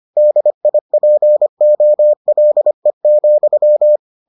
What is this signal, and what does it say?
Morse code, heard as a keyed tone, about 25 wpm, 600 Hz. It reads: DIPOLE,